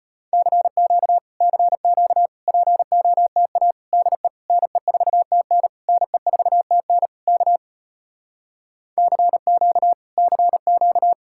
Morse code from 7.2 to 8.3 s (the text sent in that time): K